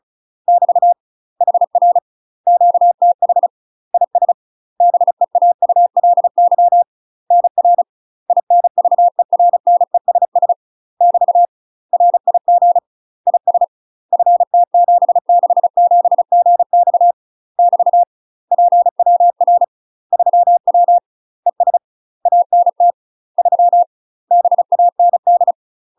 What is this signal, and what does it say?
Morse code, heard as a keyed tone, about 36 wpm, 700 Hz. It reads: <BT> HR QTH IS BEAULY NR INVERNESS <BT> RIG IS FT767GX <BT> PWR 3W ES ANT 3 BAND
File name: UK MASTER QSOs TRAINING / 35WPM / UK MASTER QSOs 0013